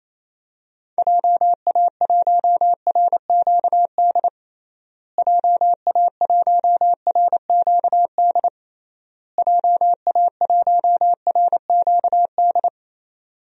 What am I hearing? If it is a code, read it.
Morse code, heard as a keyed tone, about 28 wpm, 700 Hz. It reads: JA1RQD JA1RQD JA1RQD
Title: DXCC-100 Call Signs - 6 Characters - Repeated 3x - 28wpm